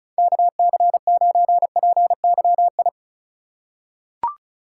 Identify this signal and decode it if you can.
Morse code, heard as a keyed tone, about 34 wpm, 700 Hz. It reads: KC9PYI
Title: USA Call Signs - 6 Characters - 2x Character Spacing - 35wpm